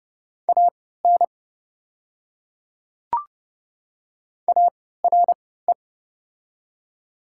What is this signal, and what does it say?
Morse code, heard as a keyed tone, about 30 wpm, spaced as for 17 wpm, 700 Hz. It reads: AN ARE
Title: 3x Character Spacing - Top 100 Words - 30wpm